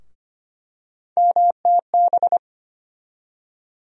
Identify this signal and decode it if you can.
Morse code, heard as a keyed tone, about 25 wpm, 700 Hz. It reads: MTB
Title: USA Call Sign Postfixes - 3 Letters 25wpm